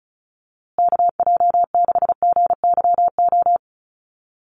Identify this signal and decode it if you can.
Morse code, heard as a keyed tone, about 34 wpm, 700 Hz. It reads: KJ6GYO
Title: USA Call Signs Repeated 2x 35wpm